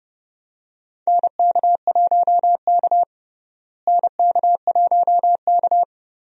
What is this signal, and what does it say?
Morse code, heard as a keyed tone, about 29 wpm, 700 Hz. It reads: NK1K NK1K
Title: USA Call Signs - 4 Characters - Repeated 2x - 30wpm